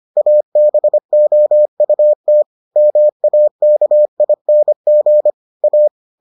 Morse code, heard as a keyed tone, about 25 wpm, 600 Hz. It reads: ABOUT MAKING A